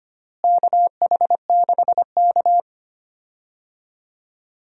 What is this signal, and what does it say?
Morse code, heard as a keyed tone, about 25 wpm, 700 Hz. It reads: KH6K
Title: DXCC-100 Call Signs - 4 Characters - Repeated 2x - 25wpm